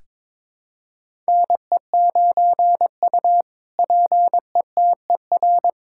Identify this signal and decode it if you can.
Morse code, heard as a keyed tone, about 22 wpm, 700 Hz. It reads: NE9U PETER
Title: K1USN SST QSO Practice - 22wpm